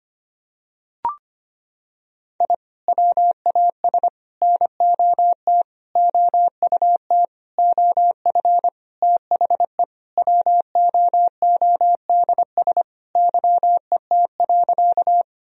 Morse code, heard as a keyed tone, about 25 wpm, 700 Hz. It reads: I WAS NOT OUT OF THE WOODS YET.
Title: Sentences from Top 1000 Words - Review Part 2 - 25wpm